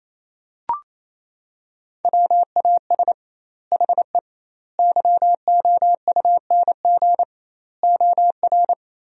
Morse code, heard as a keyed tone, about 28 wpm, 700 Hz. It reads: WAS HE YOUNG OR
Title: 2x Word-Spacing - Sentences from Top 400 Words - 28wpm